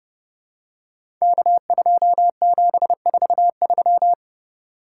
Morse code, heard as a keyed tone, about 30 wpm, 700 Hz. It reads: K2743
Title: Code Groups - 5 Letter:Numbers - 30wpm